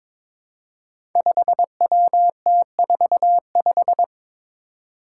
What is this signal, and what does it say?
Morse code, heard as a keyed tone, about 22 wpm, 700 Hz. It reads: 5WT45
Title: Five Challenging Letter-Number Combinations 22wpm